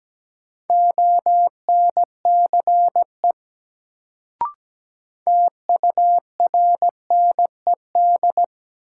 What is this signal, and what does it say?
Morse code, heard as a keyed tone, about 17 wpm, 700 Hz. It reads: ONCE TURNED
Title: Sets of 4 Words - New Word Drill - 17wpm